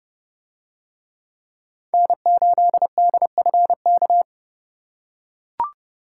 Morse code, heard as a keyed tone, about 30 wpm, 700 Hz. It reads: N8DFK